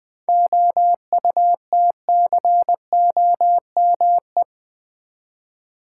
Morse code, heard as a keyed tone, about 21 wpm, 700 Hz. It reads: OUTCOME
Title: Seven Letter Words - 20wpm